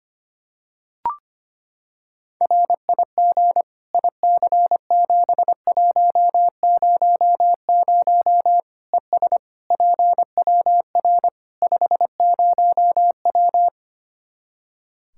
Morse code, heard as a keyed tone, about 25 wpm, 700 Hz. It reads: RIG IC7100 ES PWR 50W
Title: Practice QSO - Exchange 2 - Info - 25wpm